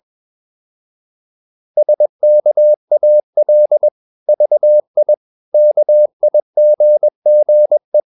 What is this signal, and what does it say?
Morse code, heard as a keyed tone, about 21 wpm, 600 Hz. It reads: SKAL VI KIGGE
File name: Ferie QRS 21wpm